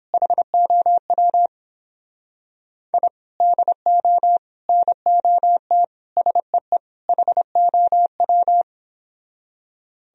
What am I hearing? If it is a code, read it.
Morse code, heard as a keyed tone, about 29 wpm, 700 Hz. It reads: HOW I DO NOT SEE HOW